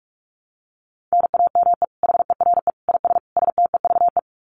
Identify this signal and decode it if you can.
Morse code, heard as a keyed tone, about 44 wpm, 700 Hz. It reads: NAME HERE IS STEVE
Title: Common QSO Phrases 45wpm